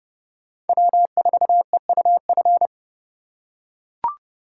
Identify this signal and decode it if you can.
Morse code, heard as a keyed tone, about 29 wpm, 700 Hz. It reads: W4EUF